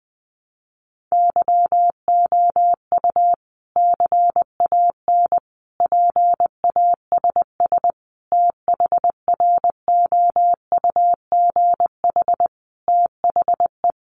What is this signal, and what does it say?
Morse code, heard as a keyed tone, about 20 wpm, 700 Hz. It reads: YOU CAN PASS THROUGH THE
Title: Sentences from Top 400 Words 20wpm